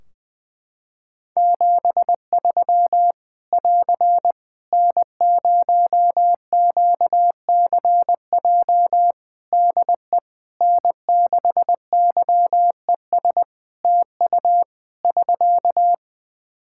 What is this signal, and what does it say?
Morse code, heard as a keyed tone, about 20 wpm, 700 Hz. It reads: 73 <AR> N0QCJ DE N6YES TU <SK>
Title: K1USN SST QSO Practice 20wpm